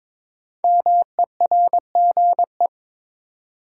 Morse code, heard as a keyed tone, about 22 wpm, 700 Hz. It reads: MERGE
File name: 5 Letter Words - First 3 Letters Given - 22wpm